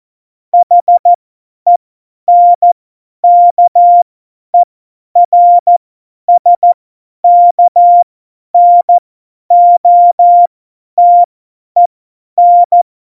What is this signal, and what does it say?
Morse code, heard as a keyed tone, about 14 wpm, spaced as for 10 wpm, 700 Hz. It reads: HENKERSKNOTEN
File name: Morsekurs / Witze / Witz 17. November 2020, 50 F70